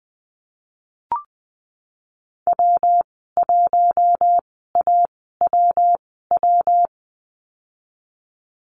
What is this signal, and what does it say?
Morse code, heard as a keyed tone, about 20 wpm, spaced as for 15 wpm, 700 Hz. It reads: W1AWW
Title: 219 Lesson 35g - Number 0 - Call Signs 20wpm